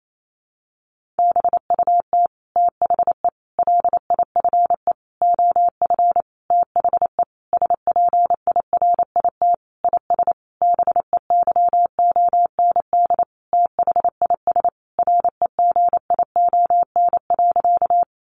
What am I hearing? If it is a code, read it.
Morse code, heard as a keyed tone, about 28 wpm, 700 Hz. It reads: BUT THE LIFE OF THE SPIRIT IS BEYOND THIS REGION.
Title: Sentences from Top 1500 Words 28wpm